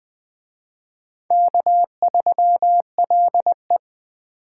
Morse code, heard as a keyed tone, about 20 wpm, 700 Hz. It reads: K3LE